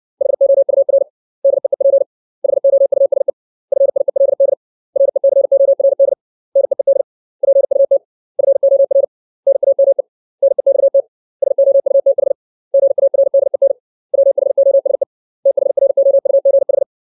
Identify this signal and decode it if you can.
Morse code, heard as a keyed tone, about 60 wpm, 550 Hz. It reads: HOUR DEEP HOUSE FIELD RECORD NEED PUT FOR NAME NEXT SOUTH GARDEN WHOSE THROUGH
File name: 632 words / 632 Words 60WPM